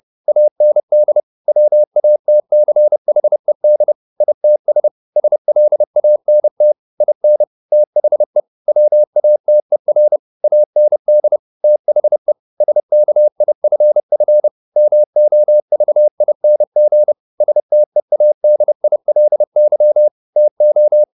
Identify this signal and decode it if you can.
Morse code, heard as a keyed tone, about 30 wpm, 600 Hz. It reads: AND WATCHED ITS SLANT IN THE WATER AND THE SKIFF MOVING STEADILY TO
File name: the old man and the sea / the old man and the sea 1 30wpm